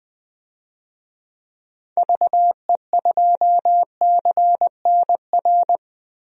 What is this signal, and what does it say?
Morse code, heard as a keyed tone, about 21 wpm, 700 Hz. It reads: VE2CNR
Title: DXCC-100 Call Signs - 6 Characters - Repeated 3x - 20wpm